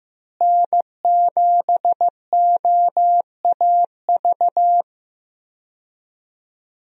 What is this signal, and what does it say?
Morse code, heard as a keyed tone, about 15 wpm, 700 Hz. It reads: N7OAV
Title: USA Call Signs - 5 Characters - Repeated 2x - 15wpm